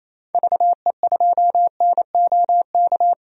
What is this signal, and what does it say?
Morse code, heard as a keyed tone, about 28 wpm, 700 Hz. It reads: VE2NOK
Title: DXCC-100 Call Signs - 6 Characters - 28wpm